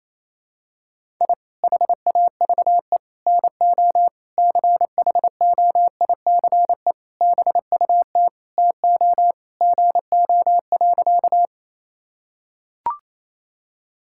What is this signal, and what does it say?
Morse code, heard as a keyed tone, about 28 wpm, 700 Hz. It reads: I HAVE NO CHOICE BUT TO GO.
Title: Sentences from Top 1000 Words - Review Part 6 - 28wpm